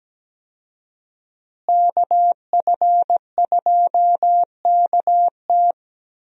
Morse code, heard as a keyed tone, about 17 wpm, 700 Hz. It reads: KF2KT
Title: USA Call Signs - 5 Characters - Repeated 2x - 17wpm